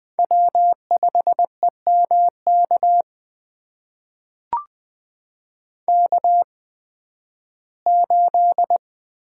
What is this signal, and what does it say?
Morse code, heard as a keyed tone, about 20 wpm, spaced as for 13 wpm, 700 Hz. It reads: W5EMK K8
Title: USA Call Signs - 5 Characters - 8x Character Spacing - 20wpm